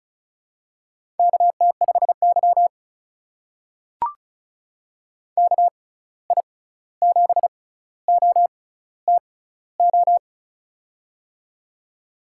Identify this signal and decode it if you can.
Morse code, heard as a keyed tone, about 36 wpm, spaced as for 16 wpm, 700 Hz. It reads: KT5Y KI7OTO
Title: USA Call Signs - 6x Character Spacing - 35wpm